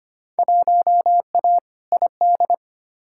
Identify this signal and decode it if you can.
Morse code, heard as a keyed tone, about 25 wpm, 700 Hz. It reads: 1A ID
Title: Field Day Exchanges Encore - 25wpm